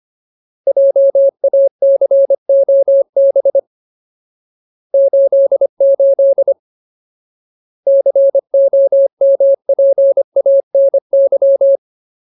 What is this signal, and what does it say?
Morse code, heard as a keyed tone, about 25 wpm, 550 Hz. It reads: JACOB 88 COMPANY
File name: Ultimate Word List / 25WPM Ultimate